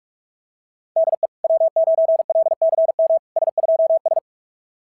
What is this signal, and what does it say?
Morse code, heard as a keyed tone, about 44 wpm, 660 Hz. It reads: DE W9LCM S2S
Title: SOTA QSO Practice 45wpm